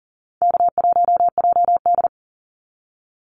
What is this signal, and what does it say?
Morse code, heard as a keyed tone, about 40 wpm, 700 Hz. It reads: K1JD
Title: NA SOTA Activator Callsigns 40wpm